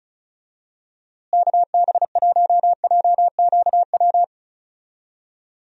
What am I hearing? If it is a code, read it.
Morse code, heard as a keyed tone, about 36 wpm, 700 Hz. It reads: KB1JQW
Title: USA Call Signs - 6 Characters - Repeated 3x - 35wpm